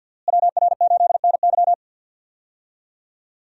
Morse code, heard as a keyed tone, about 52 wpm, 700 Hz. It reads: WR8NY
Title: USA Call Signs - 5 Characters - Repeated 2x - 50wpm